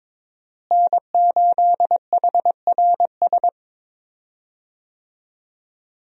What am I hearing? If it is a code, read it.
Morse code, heard as a keyed tone, about 22 wpm, 700 Hz. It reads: N8HRS
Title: USA Call Signs - 5 Characters - Repeated 3x - 22wpm